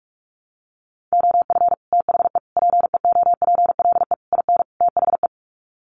Morse code, heard as a keyed tone, about 44 wpm, 700 Hz. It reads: OF THE PEOPLE IN THE
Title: Sets of 5 Words Encore 45wpm